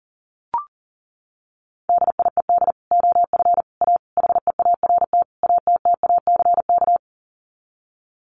Morse code, heard as a keyed tone, about 40 wpm, 700 Hz. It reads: DIED OF A HEART ATTACK
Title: Sets of 5 Words Encore2 40wpm